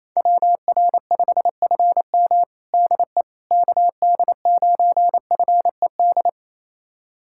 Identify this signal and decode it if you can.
Morse code, heard as a keyed tone, about 28 wpm, 700 Hz. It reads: WR5FM DE KD9FED
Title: USA Callsign DE Callsign 28wpm